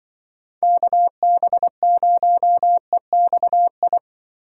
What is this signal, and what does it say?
Morse code, heard as a keyed tone, about 24 wpm, 700 Hz. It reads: KB0EXI
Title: Speed Racing - USA 2x3 Call Signs 20wpm